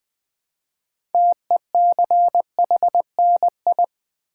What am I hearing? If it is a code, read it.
Morse code, heard as a keyed tone, about 20 wpm, 700 Hz. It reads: TECHNI
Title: Build A Word - 3 Letters At A Time - 9-Letter Words - 20wpm